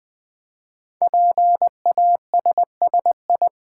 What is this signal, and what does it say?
Morse code, heard as a keyed tone, about 20 wpm, 700 Hz. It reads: PASSI
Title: Build A Word - 5 Letters At A Time - 10-Letter Words - 20wpm